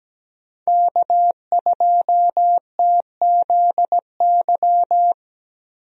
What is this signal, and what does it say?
Morse code, heard as a keyed tone, about 17 wpm, 700 Hz. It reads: K2TZY
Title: USA Call Signs - 5 Characters - Repeated 3x - 17wpm